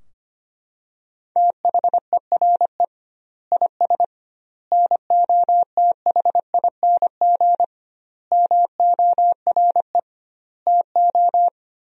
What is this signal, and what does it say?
Morse code, heard as a keyed tone, about 25 wpm, 700 Hz. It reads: THERE IS NOTHING MORE TO
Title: 2x Word-Spacing - Sentences from Top 500 Words - 25wpm